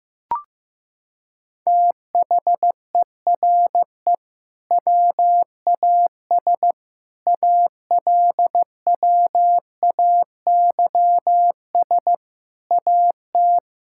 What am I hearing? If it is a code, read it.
Morse code, heard as a keyed tone, about 15 wpm, 700 Hz. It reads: THERE WAS ALWAYS AT
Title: Sentences from Top 1600 Words 15wpm